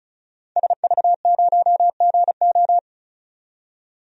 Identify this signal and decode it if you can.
Morse code, heard as a keyed tone, about 36 wpm, 700 Hz. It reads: SV0GO
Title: DXCC-100 Call Signs - 5 Characters - 35wpm